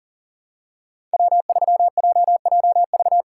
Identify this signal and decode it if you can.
Morse code, heard as a keyed tone, about 39 wpm, 700 Hz. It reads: W3JJV
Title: USA Call Signs - 5 Characters - 2x Character Spacing - 40wpm